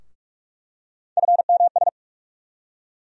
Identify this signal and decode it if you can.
Morse code, heard as a keyed tone, about 41 wpm, 700 Hz. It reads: FMS
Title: USA Call Sign Postfixes - 3 Letters 45wpm